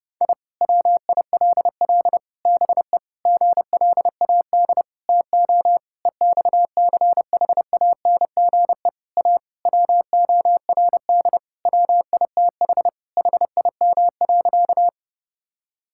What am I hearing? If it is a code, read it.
Morse code, heard as a keyed tone, about 30 wpm, 700 Hz. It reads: I WILL BE GLAD TO EXCHANGE A WORD WITH HIM.
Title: Sentences from Top 1000 Words - Review Part 6 - 30wpm